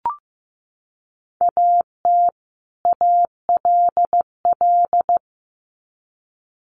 Morse code, heard as a keyed tone, about 15 wpm, 700 Hz. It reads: AT ALL
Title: Sets of 2 Words 15wpm